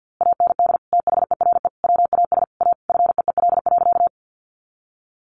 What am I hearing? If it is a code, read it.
Morse code, heard as a keyed tone, about 50 wpm, 700 Hz. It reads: AND THERE WAS A FEEL.
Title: A Sound Of Thunder 50wpm